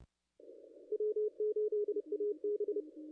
Morse code, heard as a keyed tone, about 30 wpm, 430 Hz. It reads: W8AB
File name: contests / ss-cw-k5tr-op-n5rz -0100